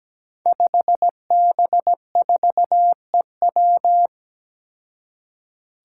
Morse code, heard as a keyed tone, about 17 wpm, 700 Hz. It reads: 5B4EW
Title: Five Challenging Letter-Number Combinations 17wpm